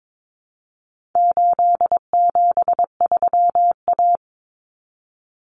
Morse code, heard as a keyed tone, about 22 wpm, 700 Hz. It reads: 873A
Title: Common Cut Numbers - 4-Digit 22wpm